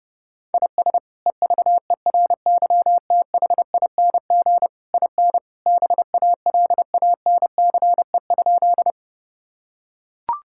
Morse code, heard as a keyed tone, about 28 wpm, 700 Hz. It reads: IS EVERYTHING IN BALANCE?
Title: Sentences from Top 1000 Words - Review Part 6 - 30wpm